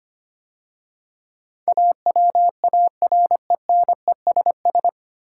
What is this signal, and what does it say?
Morse code, heard as a keyed tone, about 25 wpm, 700 Hz. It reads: AWARENESS
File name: practice / 25wpm / 9 Letter Words - First 3 Letters Given - 25wpm